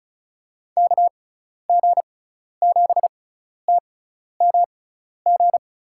Morse code, heard as a keyed tone, about 34 wpm, spaced as for 12 wpm, 700 Hz. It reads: KG7TMG
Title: USA Call Signs - 6 Characters - 6x Character Spacing - 35wpm